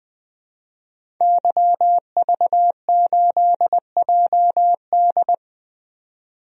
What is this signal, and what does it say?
Morse code, heard as a keyed tone, about 20 wpm, 700 Hz. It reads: YV8JD
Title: DXCC-100 Call Signs - 5 Characters - 20wpm